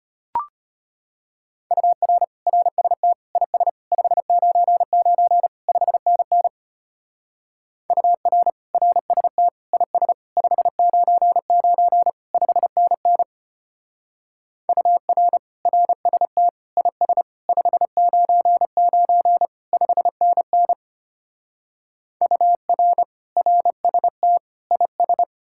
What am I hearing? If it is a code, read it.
Morse code, heard as a keyed tone, about 33 wpm, 700 Hz. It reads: UR RST IS 599 5NN UR RST IS 599 5NN UR RST IS 599 5NN UR RST IS